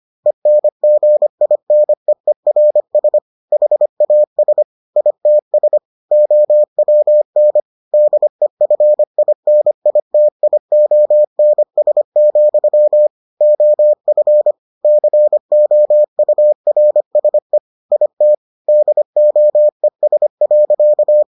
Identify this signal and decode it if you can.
Morse code, heard as a keyed tone, about 25 wpm, 600 Hz. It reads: ENGINEERS HAS ITS OWN DEFINITIONS, OF COURSE IT DOES.